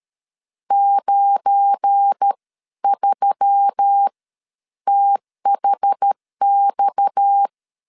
Morse code, heard as a keyed tone, about 13 wpm, spaced as for 10 wpm, 800 Hz. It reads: 93THX
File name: LEZIONI / 7 GF9Z / 7 GF9 12wpm 2319